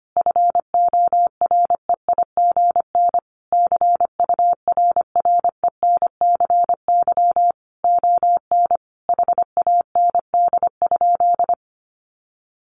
Morse code, heard as a keyed tone, about 25 wpm, 700 Hz. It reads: FOREIGN CURRENCY ON HAND?
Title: Sentences from Top 1500 Words 25wpm